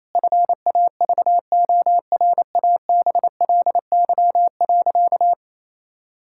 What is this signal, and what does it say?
Morse code, heard as a keyed tone, about 28 wpm, 700 Hz. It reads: FAVORABLY.
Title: Pangram Challenge v2 - 28wpm